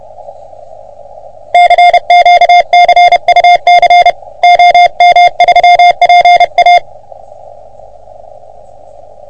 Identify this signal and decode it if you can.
Morse code, heard as a keyed tone, about 29 wpm, 720 Hz. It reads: CQ CUC OM3PA
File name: OM3PA PILUP CUC 388